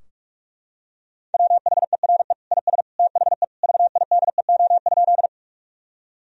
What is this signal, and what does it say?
Morse code, heard as a keyed tone, about 41 wpm, 700 Hz. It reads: WHERE IS THE VIDEO?
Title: Sentences from Top 1000 Words - Review Part 4 - 45wpm